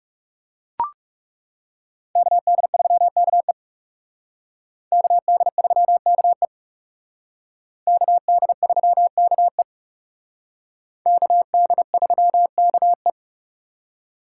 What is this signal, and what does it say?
Morse code, heard as a keyed tone, about 36 wpm, 700 Hz. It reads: KD3KE KD3KE KD3KE KD3KE
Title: Speed Racing - USA 2x2 Call Signs 30wpm